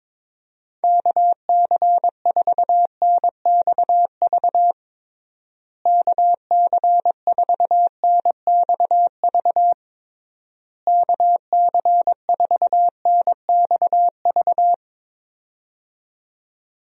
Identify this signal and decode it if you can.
Morse code, heard as a keyed tone, about 22 wpm, 700 Hz. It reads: KC4NXV KC4NXV KC4NXV
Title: USA Call Signs - 6 Characters - Repeated 3x - 22wpm